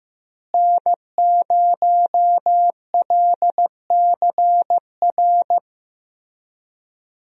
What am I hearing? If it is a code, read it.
Morse code, heard as a keyed tone, about 15 wpm, 700 Hz. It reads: N0LCR
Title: USA Call Signs - 5 Characters - Repeated 2x - 15wpm